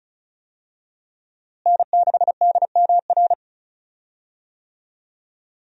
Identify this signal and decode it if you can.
Morse code, heard as a keyed tone, about 36 wpm, 700 Hz. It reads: N6DMR
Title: USA Call Signs - 5 Characters - Repeated 3x - 35wpm